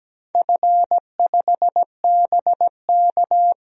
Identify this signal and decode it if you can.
Morse code, heard as a keyed tone, about 17 wpm, 700 Hz. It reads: F5BK
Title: DXCC-100 Call Signs - 4 Characters - 17wpm